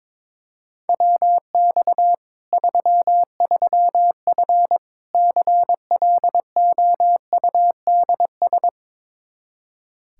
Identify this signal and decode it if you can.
Morse code, heard as a keyed tone, about 22 wpm, 700 Hz. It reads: WX 33F CLOUDS
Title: Practice QSO - Exchange 2 - Info - 22wpm